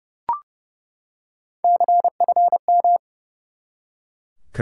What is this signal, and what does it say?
Morse code, heard as a keyed tone, about 30 wpm, 700 Hz. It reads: CFM
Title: Top 90 QSO Elements 30wpm